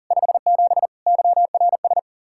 Morse code, heard as a keyed tone, about 51 wpm, 700 Hz. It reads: 57 YRS